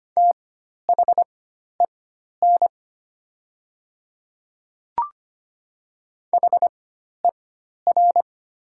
Morse code, heard as a keyed tone, about 25 wpm, spaced as for 12 wpm, 700 Hz. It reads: THEN HER
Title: 4x Character Spacing - Top 100 Words - 25wpm